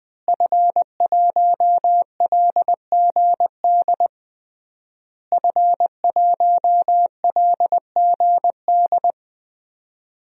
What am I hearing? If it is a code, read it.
Morse code, heard as a keyed tone, about 21 wpm, 700 Hz. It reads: F1LGD F1LGD
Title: DXCC-100 Call Signs - 5 Characters - Repeated 3x - 20wpm